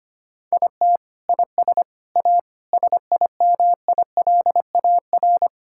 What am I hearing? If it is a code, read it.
Morse code, heard as a keyed tone, about 25 wpm, 700 Hz. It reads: IT IS A SIMILAR
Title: Sentences from Top 1000 Words - Review Part 5 - 25wpm